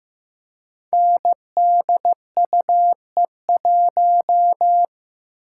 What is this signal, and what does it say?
Morse code, heard as a keyed tone, about 15 wpm, 700 Hz. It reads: NDUE1
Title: Code Groups - 5 Letter:Numbers - 15wpm